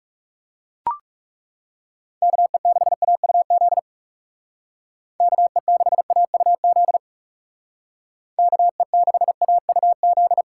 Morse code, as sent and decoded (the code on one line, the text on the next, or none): -.- . -.... .- ..- --.. / -.- . -.... .- ..- --.. / -.- . -.... .- ..- --..
KE6AUZ KE6AUZ KE6AUZ